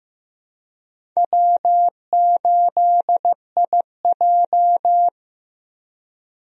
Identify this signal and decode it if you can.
Morse code, heard as a keyed tone, about 16 wpm, 700 Hz. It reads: W8IJ
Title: USA Call Signs - 4 Characters - Repeated 3x - 15wpm